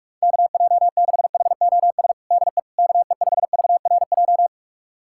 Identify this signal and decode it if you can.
Morse code, heard as a keyed tone, about 42 wpm, 700 Hz. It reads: KJ6HOS DE KE5VRJ
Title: USA Callsign DE Callsign - 6 Characters - 45wpm